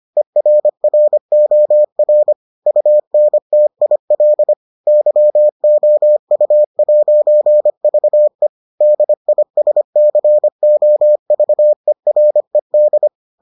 Morse code, heard as a keyed tone, about 25 wpm, 600 Hz. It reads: ERROR UNTIL YOU'VE DISCOVERED